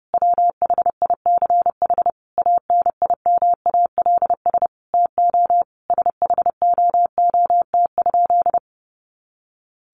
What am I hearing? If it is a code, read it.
Morse code, heard as a keyed tone, about 28 wpm, 700 Hz. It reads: WHICH ANIMALS TO SHOOT?
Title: A Sound Of Thunder 30wpm